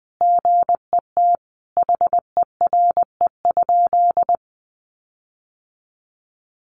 Morse code, heard as a keyed tone, about 20 wpm, 700 Hz. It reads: GET HERE?
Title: Sentences from Top 1400 Words 20wpm